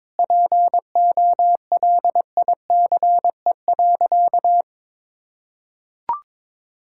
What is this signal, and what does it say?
Morse code, heard as a keyed tone, about 22 wpm, 700 Hz. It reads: POLICE.
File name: Sentences with 7-Word Phrases - 22wpm